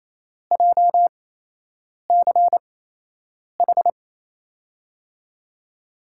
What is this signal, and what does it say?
Morse code, heard as a keyed tone, about 28 wpm, spaced as for 8 wpm, 700 Hz. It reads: JCH
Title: USA Call Signs - 6 Characters - 8x Character Spacing - 28wpm